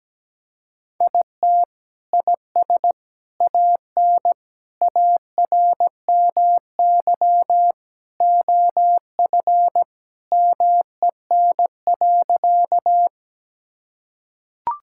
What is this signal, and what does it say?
Morse code, heard as a keyed tone, about 17 wpm, 700 Hz. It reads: IT IS AN ARMY OF MEN.
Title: Sentences from Top 1000 Words - Review Part 5 - 17wpm